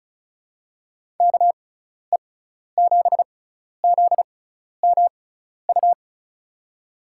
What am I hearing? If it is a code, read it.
Morse code, heard as a keyed tone, about 36 wpm, spaced as for 12 wpm, 690 Hz. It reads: KE7ZMU
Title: USA Call Signs - 6 Characters - 6x Character Spacing - 35wpm